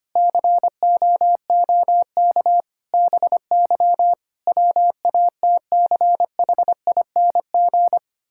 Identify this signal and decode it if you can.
Morse code, heard as a keyed tone, about 25 wpm, 700 Hz. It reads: COOK BY WATCHING
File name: practice / 25wpm / Difficult Sentences with Hints - 25wpm